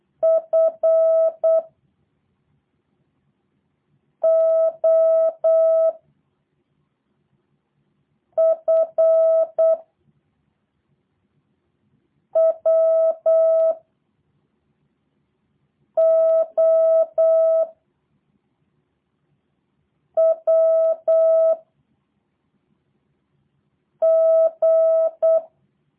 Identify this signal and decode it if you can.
Morse code, heard as a keyed tone, about 8 wpm, 650 Hz. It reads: F O F W O W G
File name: jh8cbh / 2023cw / review1-4